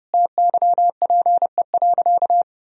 Morse code, heard as a keyed tone, about 30 wpm, 700 Hz. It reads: TYPE.
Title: Sentences from Top 1000 Words - Review Part 3 - 30wpm